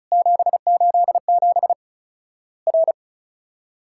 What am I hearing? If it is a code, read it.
Morse code, heard as a keyed tone, about 34 wpm, 700 Hz. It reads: 787 R
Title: Worked All Europe DX Contest - QTC Exchange - 35wpm